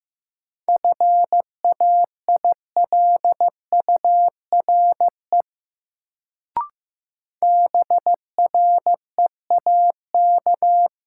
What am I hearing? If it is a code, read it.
Morse code, heard as a keyed tone, about 16 wpm, 700 Hz. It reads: FAILURE BREAK
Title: Sentences from Top 900 Words - New Word Drill - 15wpm